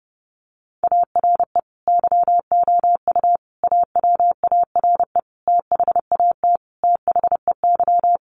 Morse code, heard as a keyed tone, about 30 wpm, 700 Hz. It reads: ARE YOU AWARE THAT THEY
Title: Sentences from Top 1200 Words 30wpm